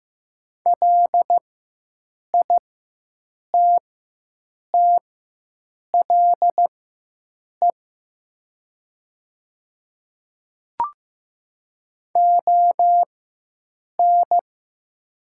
Morse code, heard as a keyed tone, about 15 wpm, spaced as for 7 wpm, 700 Hz. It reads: LITTLE ON
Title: 4x Character Spacing - Top 200 Words - 15wpm